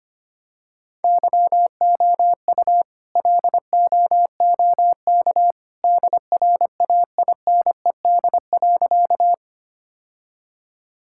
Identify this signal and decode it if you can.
Morse code, heard as a keyed tone, about 25 wpm, 700 Hz. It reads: YOU LOOK DRAINED.
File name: practice / 25wpm / All-Age Appropriate Jokes - 25wpm